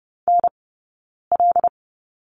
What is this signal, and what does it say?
Morse code, heard as a keyed tone, about 30 wpm, spaced as for 9 wpm, 690 Hz. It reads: NL
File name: Lesson 17d - Letter u - Two Characters 30wpm